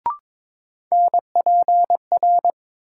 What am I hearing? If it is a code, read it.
Morse code, heard as a keyed tone, about 22 wpm, 700 Hz. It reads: NPR
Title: Sets of 5 Words - New Word Drill - 22wpm